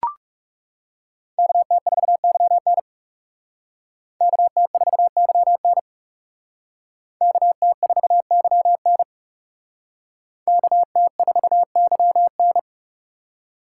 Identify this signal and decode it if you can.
Morse code, heard as a keyed tone, about 38 wpm, 700 Hz. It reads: KT4YN KT4YN KT4YN KT4YN